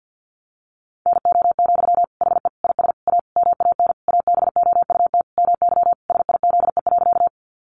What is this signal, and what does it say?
Morse code, heard as a keyed tone, about 50 wpm, 700 Hz. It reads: NO, HE IS A MAN ABOUT MY SIZE.
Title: Sentences from Top 400 Words 50wpm